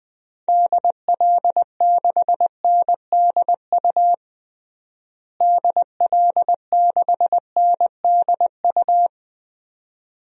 Morse code, heard as a keyed tone, about 21 wpm, 700 Hz. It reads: DL6NDU DL6NDU
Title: DXCC-100 Call Signs - 6 Characters - Repeated 2x - 20wpm